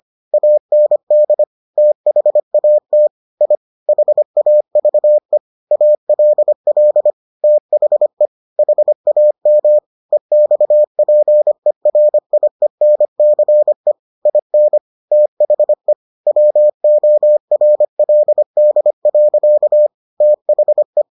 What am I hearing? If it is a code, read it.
Morse code, heard as a keyed tone, about 25 wpm, 600 Hz. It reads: AND THAT I HAVE ALL THE HAM EXPERIENCE IN THE WORLD. THE